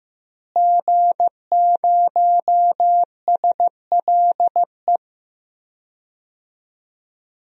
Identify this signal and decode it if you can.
Morse code, heard as a keyed tone, about 15 wpm, 700 Hz. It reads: G0SLE
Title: DXCC-100 Call Signs - 5 Characters - Repeated 3x - 15wpm